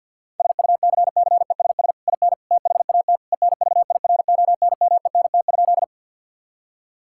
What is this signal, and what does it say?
Morse code, heard as a keyed tone, about 50 wpm, 700 Hz. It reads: SUCCESS IN THAT ENVIRONMENT?
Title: Sentences from Top 1000 Words - Review Part 5 - 50wpm